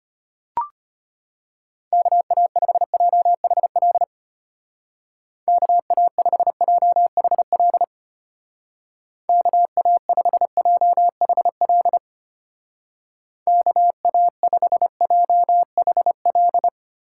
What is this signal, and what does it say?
Morse code, heard as a keyed tone, about 31 wpm, 700 Hz. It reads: KA5JHL KA5JHL KA5JHL KA5JHL
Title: Speed Racing - USA 2x3 Call Signs 25wpm